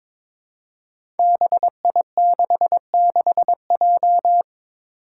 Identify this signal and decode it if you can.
Morse code, heard as a keyed tone, about 22 wpm, 700 Hz. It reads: BI66J
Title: Five Challenging Letter-Number Combinations 22wpm